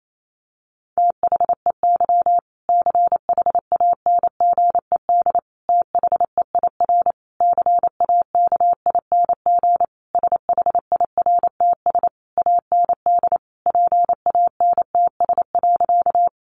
Morse code, heard as a keyed tone, about 28 wpm, 700 Hz. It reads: THEY CHANGED THEIR CAKING SHIRTS AND PANTS.
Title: A Sound Of Thunder 28wpm